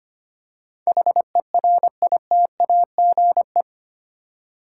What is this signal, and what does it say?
Morse code, heard as a keyed tone, about 25 wpm, 700 Hz. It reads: HERITAGE
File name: 8 Letter Words - First 3 Letters Given - 25wpm